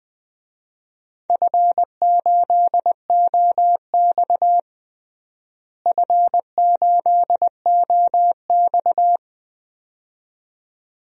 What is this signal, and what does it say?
Morse code, heard as a keyed tone, about 20 wpm, 700 Hz. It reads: F8OX F8OX
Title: DXCC-100 Call Signs - 4 Characters - Repeated 2x - 20wpm